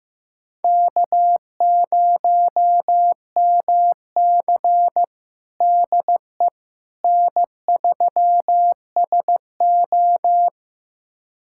Morse code, heard as a keyed tone, about 15 wpm, 700 Hz. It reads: K0MC DE N3SO
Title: USA Callsign DE Callsign - 4 Characters - 15wpm